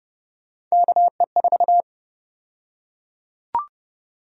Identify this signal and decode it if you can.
Morse code, heard as a keyed tone, about 30 wpm, 700 Hz. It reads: KE4
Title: USA Call Sign Prefixes - 30wpm